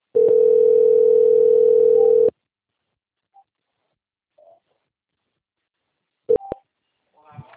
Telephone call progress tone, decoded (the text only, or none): ringback tone